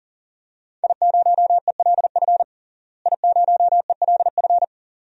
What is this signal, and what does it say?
Morse code, heard as a keyed tone, about 40 wpm, 700 Hz. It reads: I0ELF I0ELF
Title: DXCC-100 Call Signs - 5 Characters - Repeated 2x - 40wpm